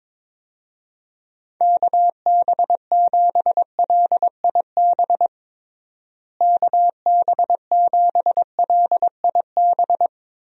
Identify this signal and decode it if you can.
Morse code, heard as a keyed tone, about 22 wpm, 700 Hz. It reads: KB7LIB KB7LIB
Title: USA Call Signs - 6 Characters - Repeated 2x - 22wpm